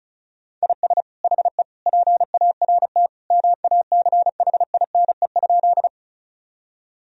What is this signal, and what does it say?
Morse code, heard as a keyed tone, about 36 wpm, 700 Hz. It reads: IS HE PART MACHINE?
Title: Sentences from Top 1000 Words - Review Part 2 - 35wpm